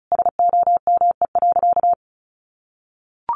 Morse code, heard as a keyed tone, about 34 wpm, 700 Hz. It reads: SOME.
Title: Sentences from Top 800 Words 35wpm